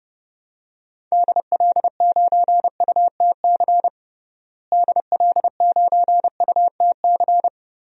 Morse code, heard as a keyed tone, about 31 wpm, 700 Hz. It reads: DL9UTC DL9UTC
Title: DXCC-100 Call Signs - 6 Characters - Repeated 3x - 30wpm